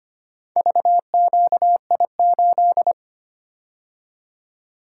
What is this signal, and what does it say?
Morse code, heard as a keyed tone, about 25 wpm, 700 Hz. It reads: VQI8
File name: practice / 25wpm / Four Letter-Number Combinations - 25wpm